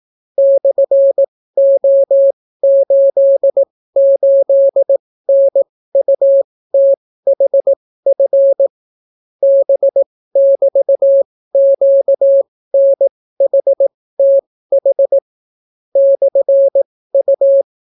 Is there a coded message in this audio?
Yes: Morse code from 0.4 to 17.6 s, heard as a keyed tone, about 18 wpm, spaced as for 14 wpm, 550 Hz. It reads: /O88NUTHF B<BT>QNHTH /U